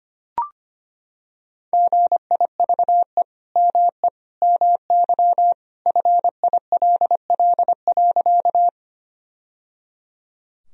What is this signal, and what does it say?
Morse code, heard as a keyed tone, about 25 wpm, 700 Hz. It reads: GIVE ME MY FILL.
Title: Sentences from Top 1000 Words - Review Part 3 - 25wpm